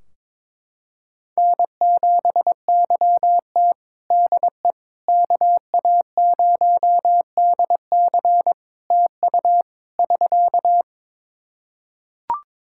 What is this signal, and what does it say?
Morse code, heard as a keyed tone, about 23 wpm, 700 Hz. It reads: N7YT DE KA0DC TU <SK>
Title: Practice QSO - Exchange 4 - Ending - 22wpm